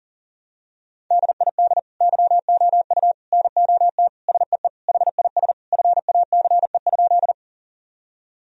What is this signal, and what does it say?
Morse code, heard as a keyed tone, about 40 wpm, 700 Hz. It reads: DID YOU NOT SEE HIS FACE?
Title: Sentences from Top 1000 Words - Review Part 2 - 40wpm